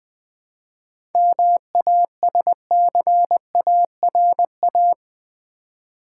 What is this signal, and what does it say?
Morse code, heard as a keyed tone, about 20 wpm, 700 Hz. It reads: MASCARA
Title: Easy Words with Hints - 20wpm